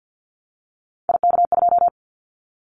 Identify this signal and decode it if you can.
Morse code, heard as a keyed tone, about 50 wpm, 700 Hz. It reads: IK2
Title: Callsign Prefixes for Top 50 DXCC - 50wpm